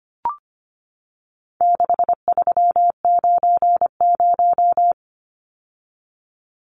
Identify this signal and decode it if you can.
Morse code, heard as a keyed tone, about 25 wpm, 700 Hz. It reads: 6390
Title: Numbers - 4-Digit 25wpm